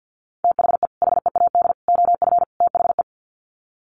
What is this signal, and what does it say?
Morse code, heard as a keyed tone, about 49 wpm, 700 Hz. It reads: THE HEAD OF THE
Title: Sets of 4 Words 50wpm